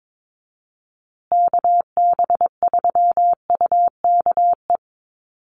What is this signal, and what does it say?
Morse code, heard as a keyed tone, about 22 wpm, 700 Hz. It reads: KB3UKE
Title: USA Call Signs Repeated 2x 22wpm